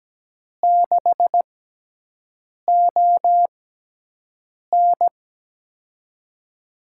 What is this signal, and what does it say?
Morse code, heard as a keyed tone, about 17 wpm, spaced as for 6 wpm, 700 Hz. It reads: 6ON
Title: USA Call Signs - 6x Character Spacing - 17wpm